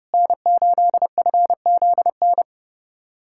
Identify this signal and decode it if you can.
Morse code, heard as a keyed tone, about 32 wpm, 700 Hz. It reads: N8FZN
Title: USA Callsign DE Callsign - 5 Characters - 30wpm